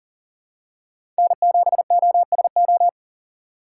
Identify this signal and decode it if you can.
Morse code, heard as a keyed tone, about 40 wpm, 700 Hz. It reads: N7OSO
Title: USA Call Signs - 5 Characters - 3x Character Spacing - 40wpm